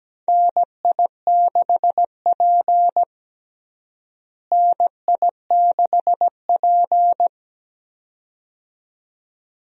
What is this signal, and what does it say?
Morse code, heard as a keyed tone, about 17 wpm, 700 Hz. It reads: NI6P NI6P
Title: USA Call Signs - 4 Characters - Repeated 2x - 17wpm